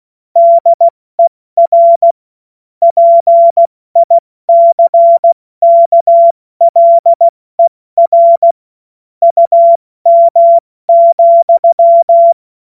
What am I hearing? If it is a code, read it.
Morse code, heard as a keyed tone, about 16 wpm, spaced as for 14 wpm, 680 Hz. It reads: DER PICKLER UM,
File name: Morsekurs / Witze / Witz 19 Dezember 2023 70 F80